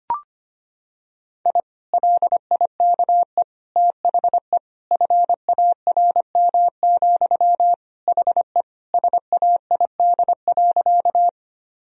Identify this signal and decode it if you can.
Morse code, heard as a keyed tone, about 25 wpm, 700 Hz. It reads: I LIKE THE FARM, HE SAID.
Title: Sentences from Top 1000 Words - Review Part 2 - 25wpm